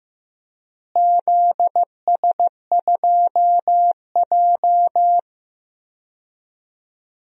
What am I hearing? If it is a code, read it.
Morse code, heard as a keyed tone, about 15 wpm, 700 Hz. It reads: ZS2J
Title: DXCC-100 Call Signs - 4 Characters - Repeated 3x - 15wpm